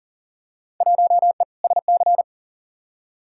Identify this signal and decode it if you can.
Morse code, heard as a keyed tone, about 40 wpm, 700 Hz. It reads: 1E SC
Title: Field Day Exchanges - 40wpm